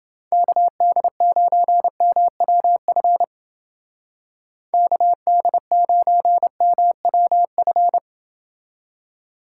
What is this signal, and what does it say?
Morse code, heard as a keyed tone, about 30 wpm, 700 Hz. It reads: KD9MWF KD9MWF